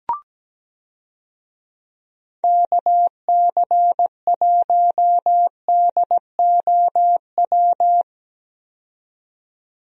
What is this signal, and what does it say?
Morse code, heard as a keyed tone, about 17 wpm, 700 Hz. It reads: KC1DOW
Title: USA Call Signs - 6 Characters - 17wpm